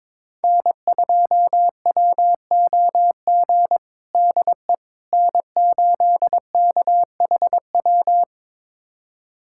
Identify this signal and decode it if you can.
Morse code, heard as a keyed tone, about 21 wpm, 700 Hz. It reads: N2WOG DE N8KHW
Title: USA Callsign DE Callsign - 5 Characters - 22wpm